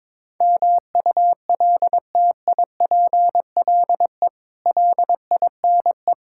Morse code, heard as a keyed tone, about 22 wpm, 700 Hz. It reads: MULTIPLE LINE
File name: practice / 22wpm / Build A Phrase - 5-Word Phrases - 22wpm